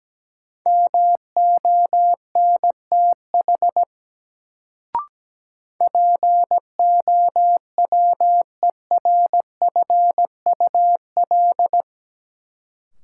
Morse code, heard as a keyed tone, about 17 wpm, 700 Hz. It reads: MONTH POWERFUL
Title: Sets of 5 Words Encore - New Word Drill - 17wpm